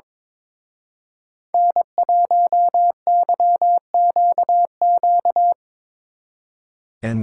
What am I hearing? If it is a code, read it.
Morse code, heard as a keyed tone, about 24 wpm, 700 Hz. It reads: N1YQQ
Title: Speed Racing - USA 1x3 Call Signs 22wpm